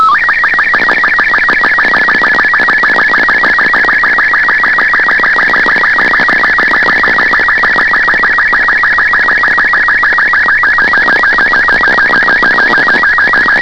SSTV (slow-scan television), Robot 36.